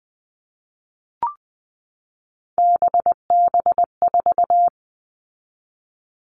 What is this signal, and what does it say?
Morse code, heard as a keyed tone, about 20 wpm, 700 Hz. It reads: BB4
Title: All Cut Numbers - 3-Digit 20wpm